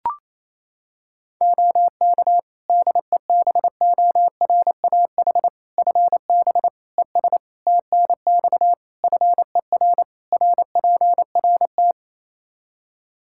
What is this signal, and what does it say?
Morse code, heard as a keyed tone, about 28 wpm, 700 Hz. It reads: OK DEBORAH FB ES TNX FER RPRT
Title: Practice QSO - Exchange 2 - Info - 28wpm